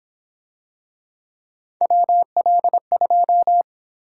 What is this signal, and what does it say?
Morse code, heard as a keyed tone, about 26 wpm, 700 Hz. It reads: WL2
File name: Speed-Racing - Three Letter-Number Combinations - 22wpm